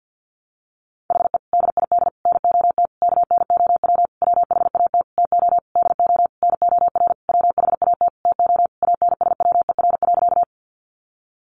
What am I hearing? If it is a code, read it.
Morse code, heard as a keyed tone, about 50 wpm, 700 Hz. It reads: HE DID NOT KNOW WHAT TO DO NOR WHAT TO ANSWER.
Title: Sentences from Top 1500 Words 50wpm